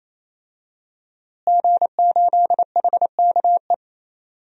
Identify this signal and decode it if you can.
Morse code, heard as a keyed tone, about 28 wpm, 700 Hz. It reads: G8HKE
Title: DXCC-100 Call Signs - 5 Characters - Repeated 3x - 28wpm